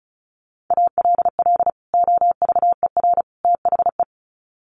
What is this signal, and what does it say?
Morse code, heard as a keyed tone, about 34 wpm, 700 Hz. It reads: ALL OVER THE
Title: Sets of 3 Words 35wpm